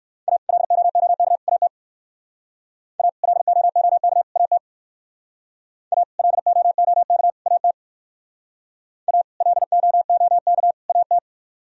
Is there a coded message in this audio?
Yes: Morse code from 0.3 to 11.2 s, heard as a keyed tone, about 58 wpm, 700 Hz. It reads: A LOOK AT A LOOK AT A LOOK AT A LOOK AT